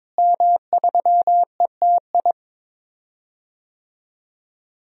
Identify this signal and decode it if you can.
Morse code, heard as a keyed tone, about 22 wpm, 700 Hz. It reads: M3ETI
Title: DXCC-100 Call Signs - 5 Characters - Repeated 3x - 22wpm